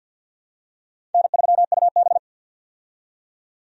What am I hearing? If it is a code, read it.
Morse code, heard as a keyed tone, about 50 wpm, 700 Hz. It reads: N3UB
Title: USA Call Signs - 4 Characters - 50wpm